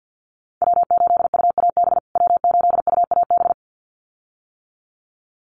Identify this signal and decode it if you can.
Morse code, heard as a keyed tone, about 50 wpm, 700 Hz. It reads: W8UAB W8UAB
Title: USA Call Signs Repeated 2x 50wpm